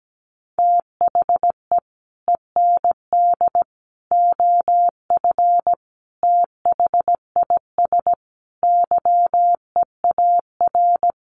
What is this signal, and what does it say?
Morse code, heard as a keyed tone, about 17 wpm, 700 Hz. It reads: THE END OF THIS YEAR
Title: Sets of 5 Words Encore 17wpm